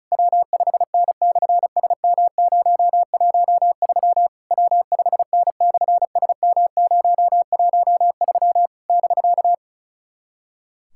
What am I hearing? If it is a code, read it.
Morse code, heard as a keyed tone, about 34 wpm, 700 Hz. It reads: W5N/SM013 W5N/SM013 <BK>
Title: SOTA QSO Practice 35wpm